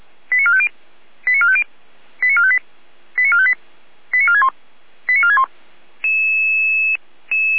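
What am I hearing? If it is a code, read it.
Five-tone selective call (ZVEI), heard as quick runs of five tones, tones of about 70 ms. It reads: 89450 89450 89458 89458 89461 89461
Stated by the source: ZVEI five-tone selective call received by SDR, from the Feuerwehr Orlamünde fire brigade audio server.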